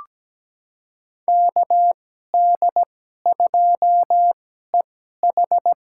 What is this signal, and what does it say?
Morse code, heard as a keyed tone, about 17 wpm, spaced as for 12 wpm, 700 Hz. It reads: KD2EH
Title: USA Call Signs - 2x Character Spacing - 17wpm